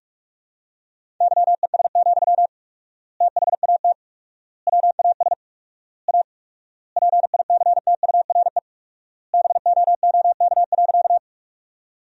Morse code, heard as a keyed tone, about 47 wpm, 690 Hz. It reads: YES, THAT WAS A PICTURE BOOK.
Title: 4x Word-Spacing - Sentences from Top 500 Words - 45wpm